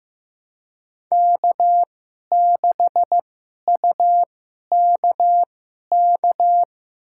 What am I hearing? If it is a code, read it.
Morse code, heard as a keyed tone, about 15 wpm, spaced as for 11 wpm, 700 Hz. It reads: K6UKK
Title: USA Call Signs - 5 Characters - 2x Character Spacing - 15wpm